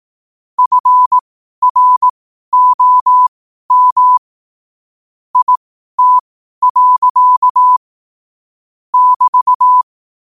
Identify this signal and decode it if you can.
Morse code, heard as a keyed tone, about 18 wpm, spaced as for 12 wpm, 1000 Hz. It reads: FROM IT. <BT>
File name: qotdfiles / 12wpm